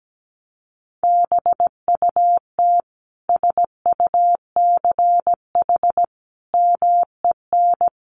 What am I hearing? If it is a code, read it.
Morse code, heard as a keyed tone, about 17 wpm, 700 Hz. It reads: BUT SUCH MEN
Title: Sentences from Top 1100 Words 17wpm